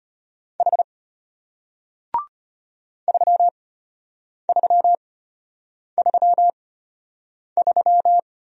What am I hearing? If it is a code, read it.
Morse code, heard as a keyed tone, about 33 wpm, spaced as for 8 wpm, 690 Hz. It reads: H 3333